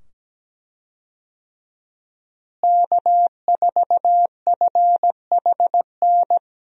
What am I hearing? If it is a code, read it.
Morse code, heard as a keyed tone, about 17 wpm, 700 Hz. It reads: K4FHN
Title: USA Call Signs - 5 Characters - Repeated 2x - 17wpm